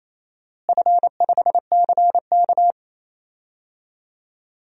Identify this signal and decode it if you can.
Morse code, heard as a keyed tone, about 27 wpm, 700 Hz. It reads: F5CK
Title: DXCC-100 Call Signs - 4 Characters - Repeated 3x - 28wpm